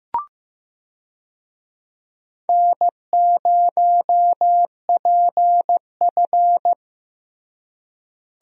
Morse code, heard as a keyed tone, about 15 wpm, 700 Hz. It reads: N0PF
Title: USA Call Signs - 4 Characters - Repeated 2x - 15wpm